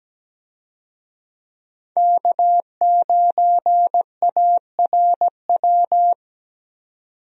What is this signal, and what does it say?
Morse code, heard as a keyed tone, about 17 wpm, 700 Hz. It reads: K9ARW
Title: USA Call Signs - 5 Characters - 17wpm